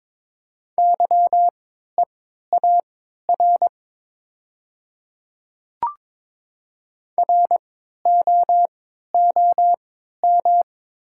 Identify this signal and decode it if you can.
Morse code, heard as a keyed tone, about 22 wpm, spaced as for 13 wpm, 700 Hz. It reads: YEAR ROOM
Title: 3x Character Spacing - Top 300 Words - 22wpm